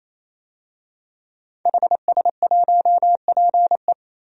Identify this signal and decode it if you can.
Morse code, heard as a keyed tone, about 28 wpm, 700 Hz. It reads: HS1PE
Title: DXCC-100 Call Signs - 5 Characters - 28wpm